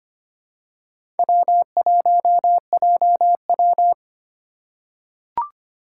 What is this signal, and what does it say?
Morse code, heard as a keyed tone, about 25 wpm, 700 Hz. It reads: W1JW
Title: USA Call Signs - 3x Character Spacing - 25wpm